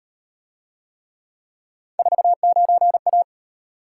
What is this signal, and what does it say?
Morse code, heard as a keyed tone, about 37 wpm, 700 Hz. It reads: 49A